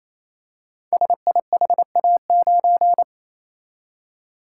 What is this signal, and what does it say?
Morse code, heard as a keyed tone, about 27 wpm, 700 Hz. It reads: SIHA9
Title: Code Groups - 5 Letter:Numbers - 28wpm